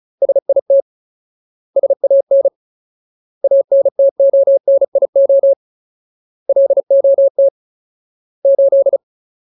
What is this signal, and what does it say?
Morse code, heard as a keyed tone, about 34 wpm, 550 Hz. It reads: SIT SAN ANTONIO LOT 8
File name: Ultimate Word List / 35WPM Ultimate